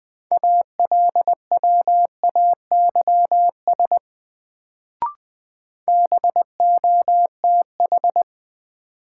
Morse code, heard as a keyed tone, about 20 wpm, 700 Hz. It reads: ALWAYS BOTH
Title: Sentences from Top 300 Words - New Word Drill - 20wpm